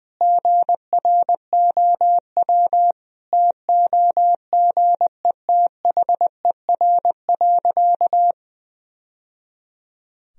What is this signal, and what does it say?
Morse code, heard as a keyed tone, about 20 wpm, 700 Hz. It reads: GROW TOGETHER.
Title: Sentences from Top 1000 Words - Review Part 1 - 20wpm